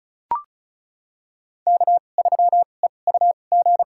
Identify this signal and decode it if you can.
Morse code, heard as a keyed tone, about 36 wpm, spaced as for 25 wpm, 700 Hz. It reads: K3EUG
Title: USA Call Signs - 2x Character Spacing - 35wpm